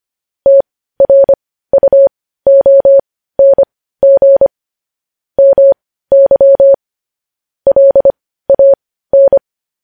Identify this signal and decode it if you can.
Morse code, heard as a keyed tone, about 25 wpm, spaced as for 15 wpm, 550 Hz. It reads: TRUONG MY LAN